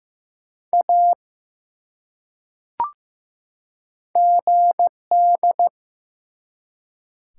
Morse code, heard as a keyed tone, about 15 wpm, 700 Hz. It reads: A GD
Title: Top 180 QSO Elements 15wpm